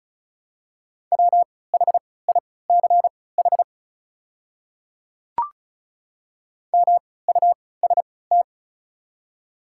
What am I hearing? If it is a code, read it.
Morse code, heard as a keyed tone, about 34 wpm, spaced as for 20 wpm, 700 Hz. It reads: WHICH MUST
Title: 3x Character Spacing - Top 300 Words - 35wpm